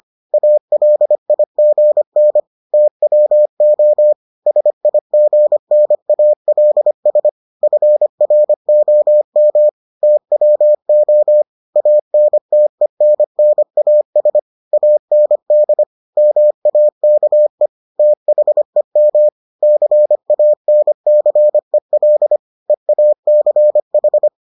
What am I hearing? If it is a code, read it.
Morse code, heard as a keyed tone, about 25 wpm, 600 Hz. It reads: ALIGN TWO SIGNALS FROM TWO ANTENNAS AND MAKE THEM CANCEL EACH